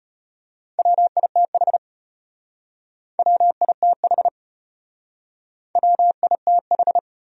Morse code, heard as a keyed tone, about 32 wpm, 700 Hz. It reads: WITH WITH WITH